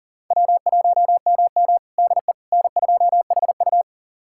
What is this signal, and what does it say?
Morse code, heard as a keyed tone, about 40 wpm, 700 Hz. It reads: W1MM DE N2HU
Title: USA Callsign DE Callsign - 4 Characters - 40wpm